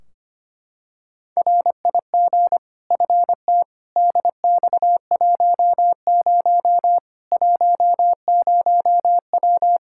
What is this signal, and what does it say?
Morse code, heard as a keyed tone, about 25 wpm, 700 Hz. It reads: RIG FT DX10 10W
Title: Practice QSO - Exchange 2 - Info - 25wpm